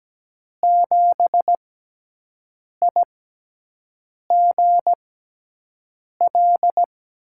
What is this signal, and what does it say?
Morse code, heard as a keyed tone, about 17 wpm, spaced as for 6 wpm, 700 Hz. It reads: 7IGL
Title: USA Call Signs - 6x Character Spacing - 17wpm